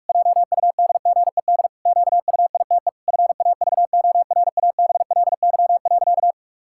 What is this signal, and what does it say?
Morse code, heard as a keyed tone, about 43 wpm, 700 Hz. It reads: JUDGED QUITE FAVORABLY.
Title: Pangram Challenge v2 - 45wpm